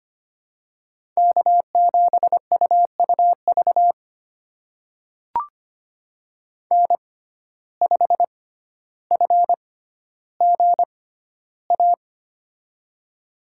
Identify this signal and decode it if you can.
Morse code, heard as a keyed tone, about 25 wpm, spaced as for 13 wpm, 700 Hz. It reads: K7UUV N5FGA
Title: USA Call Signs - 5 Characters - 6x Character Spacing - 25wpm